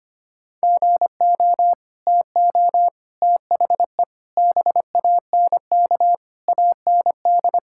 Morse code, heard as a keyed tone, about 25 wpm, 700 Hz. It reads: GO TO THE BANK AND
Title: Sentences from Top 1000 Words - Review Part 5 - 25wpm